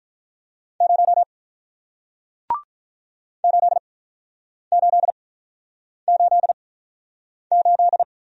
Morse code, heard as a keyed tone, about 48 wpm, spaced as for 9 wpm, 700 Hz. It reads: 0 8888